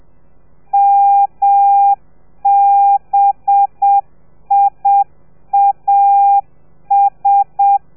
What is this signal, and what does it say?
Morse code, heard as a keyed tone, about 7 wpm, 800 Hz. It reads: MBIAS
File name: telegrafia / i2tfj / Lez4 30